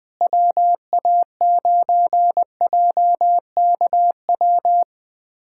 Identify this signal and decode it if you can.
Morse code, heard as a keyed tone, about 20 wpm, 700 Hz. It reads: WA9JKW
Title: USA Callsign DE Callsign - 6 Characters - 20wpm